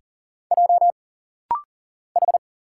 Morse code, heard as a keyed tone, about 40 wpm, 700 Hz. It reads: J H
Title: Single Letters - Rapid-Fire 40wpm